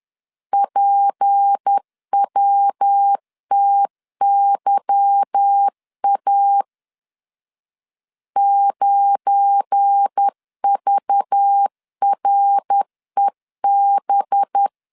Morse code, heard as a keyed tone, about 10 wpm, 800 Hz. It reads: PWTYA 9VREB
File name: LEZIONI / 9 2 10wpm 2736